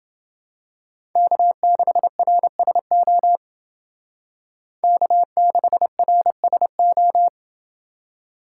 Morse code, heard as a keyed tone, about 29 wpm, 700 Hz. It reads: K6RSO K6RSO